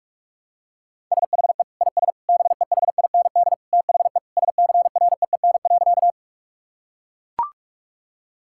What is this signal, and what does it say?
Morse code, heard as a keyed tone, about 41 wpm, 700 Hz. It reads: SHE IS BEHIND THE SCREEN.
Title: Sentences from Top 1000 Words - Review Part 6 - 45wpm